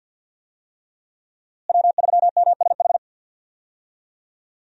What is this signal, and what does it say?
Morse code, heard as a keyed tone, about 52 wpm, 700 Hz. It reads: W3MSH
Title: USA Call Signs - 5 Characters - Repeated 3x - 50wpm